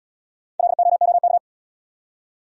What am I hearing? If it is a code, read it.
Morse code, heard as a keyed tone, about 68 wpm, 700 Hz. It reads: LOOK